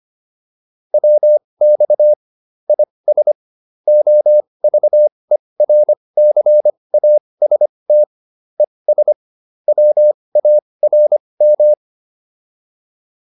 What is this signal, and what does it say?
Morse code, heard as a keyed tone, about 25 wpm, spaced as for 20 wpm, 600 Hz. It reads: WX IS OVERCAST ES WARM
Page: intermediate-practice-files